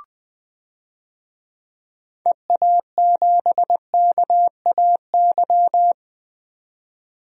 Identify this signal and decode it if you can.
Morse code, heard as a keyed tone, about 20 wpm, 700 Hz. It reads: EA7KAY
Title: DXCC-100 Call Signs - 6 Characters - 20wpm